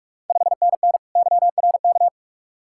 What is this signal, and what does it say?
Morse code, heard as a keyed tone, about 44 wpm, 690 Hz. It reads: 5NN YRK
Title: Ontario QSO Party Exchange 45wpm